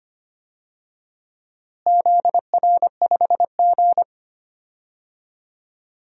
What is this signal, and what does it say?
Morse code, heard as a keyed tone, about 25 wpm, 700 Hz. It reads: ZR5G
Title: DXCC-100 Call Signs - 4 Characters - Repeated 3x - 25wpm